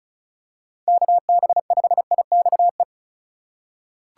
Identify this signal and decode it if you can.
Morse code, heard as a keyed tone, about 36 wpm, 700 Hz. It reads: KB5IXE
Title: USA Call Signs - 4x Character Spacing - 35wpm